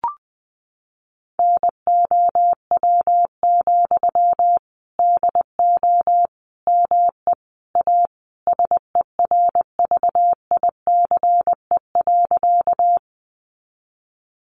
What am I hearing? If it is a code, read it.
Morse code, heard as a keyed tone, about 20 wpm, 700 Hz. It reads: NOW, DO ME A SERVICE.
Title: Sentences from Top 700 Words 20wpm